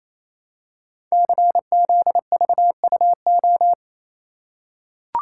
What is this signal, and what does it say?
Morse code, heard as a keyed tone, about 29 wpm, 700 Hz. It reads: CZVUO
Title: Code Groups - 5 Letter:Numbers - 28wpm